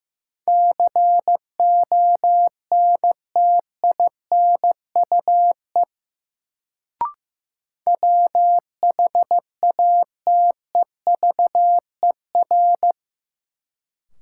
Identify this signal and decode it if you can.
Morse code, heard as a keyed tone, about 15 wpm, 700 Hz. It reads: CONTINUE WHATEVER
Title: Sentences from Top 1100 Words - New Word Drill Alt - 15wpm